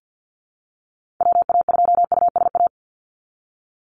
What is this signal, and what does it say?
Morse code, heard as a keyed tone, about 49 wpm, 700 Hz. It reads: WA2USA
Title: NA SOTA Chaser Callsigns 50wpm